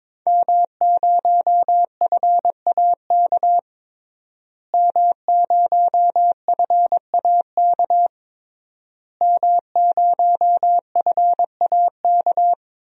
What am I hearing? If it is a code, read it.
Morse code, heard as a keyed tone, about 22 wpm, 700 Hz. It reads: M0FAK M0FAK M0FAK
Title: DXCC-100 Call Signs - 5 Characters - Repeated 3x - 22wpm